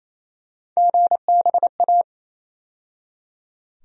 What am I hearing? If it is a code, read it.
Morse code, heard as a keyed tone, about 27 wpm, 700 Hz. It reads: GBA
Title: USA Call Sign Postfixes - 3 Letters 28wpm